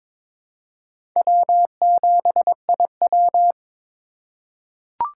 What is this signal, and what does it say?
Morse code, heard as a keyed tone, about 22 wpm, 700 Hz. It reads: W7IW
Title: USA Call Signs - 4 Characters - 2x Character Spacing - 22wpm